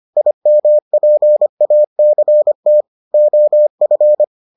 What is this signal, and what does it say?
Morse code, heard as a keyed tone, about 25 wpm, 600 Hz. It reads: IMPACT OF